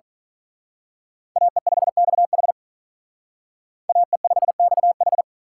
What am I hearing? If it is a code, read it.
Morse code, heard as a keyed tone, about 40 wpm, 700 Hz. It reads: AE5XH AE5XH
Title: Speed Racing - USA 2x2 Call Signs 35wpm